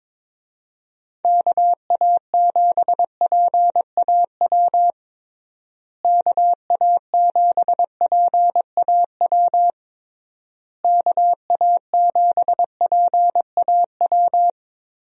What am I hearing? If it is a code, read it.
Morse code, heard as a keyed tone, about 22 wpm, 700 Hz. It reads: KA7PAW KA7PAW KA7PAW
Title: USA Call Signs - 6 Characters - Repeated 3x - 22wpm